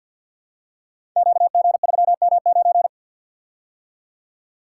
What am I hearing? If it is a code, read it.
Morse code, heard as a keyed tone, about 51 wpm, 700 Hz. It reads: QG3M9
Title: Code Groups - 5 Letter:Numbers - 50wpm